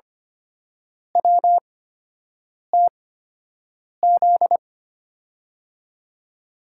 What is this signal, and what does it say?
Morse code, heard as a keyed tone, about 25 wpm, spaced as for 7 wpm, 700 Hz. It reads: WTZ
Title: USA Call Signs - 6 Characters - 8x Character Spacing - 25wpm